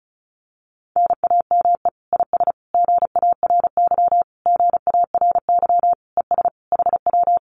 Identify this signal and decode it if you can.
Morse code, heard as a keyed tone, about 34 wpm, 700 Hz. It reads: NAME IS GARY GARY ES HW
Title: Common QSO Phrases Encore 35wpm